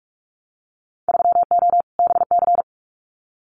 Morse code, heard as a keyed tone, about 46 wpm, 700 Hz. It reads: 3O BC
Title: Winter Field Day Exchanges Encore 45wpm